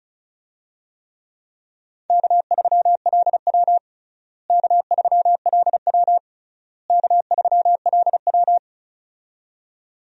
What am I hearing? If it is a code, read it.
Morse code, heard as a keyed tone, about 34 wpm, 700 Hz. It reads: K3LW K3LW K3LW
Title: USA Call Signs - 4 Characters - Repeated 3x - 35wpm